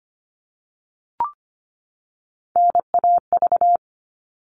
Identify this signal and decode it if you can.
Morse code, heard as a keyed tone, about 25 wpm, 700 Hz. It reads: NAV
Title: All Cut Numbers - 3-Digit 25wpm